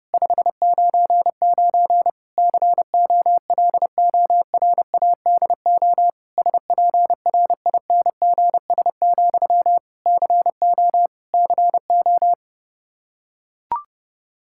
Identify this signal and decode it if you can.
Morse code, heard as a keyed tone, about 30 wpm, 700 Hz. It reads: 599 COLORADO SPRINGS, CO CO
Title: K1USN SST QSO Practice 30wpm